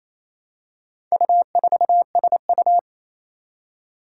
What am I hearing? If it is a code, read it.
Morse code, heard as a keyed tone, about 27 wpm, 700 Hz. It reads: U4SU
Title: Four Challenging Letter-Number Combinations 28wpm